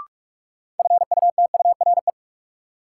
Morse code, heard as a keyed tone, about 47 wpm, 700 Hz. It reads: FUTURE
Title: Sentences from Top 700 Words - New Word Drill - 45wpm